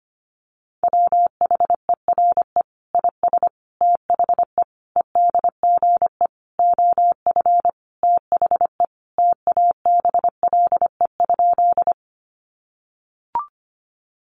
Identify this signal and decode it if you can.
Morse code, heard as a keyed tone, about 25 wpm, 700 Hz. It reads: WHERE IS THE EDGE OF THE TABLE?
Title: Sentences from Top 1300 Words 25wpm